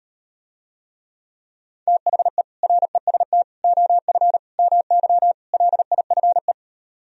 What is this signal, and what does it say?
Morse code, heard as a keyed tone, about 38 wpm, 700 Hz. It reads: THE REST OF MY LIFE